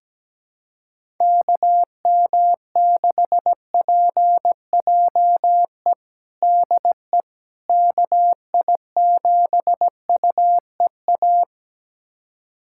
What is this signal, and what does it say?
Morse code, heard as a keyed tone, about 17 wpm, 700 Hz. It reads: KM6PJE DE KI7UEA
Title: USA Callsign DE Callsign 17wpm